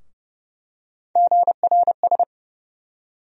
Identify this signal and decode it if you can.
Morse code, heard as a keyed tone, about 30 wpm, 700 Hz. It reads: GRS
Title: USA Call Sign Postfixes - 3 Letters 30wpm